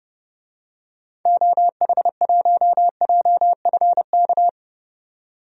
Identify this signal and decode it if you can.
Morse code, heard as a keyed tone, about 30 wpm, 700 Hz. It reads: OH1JFK
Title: DXCC-100 Call Signs - 6 Characters - 30wpm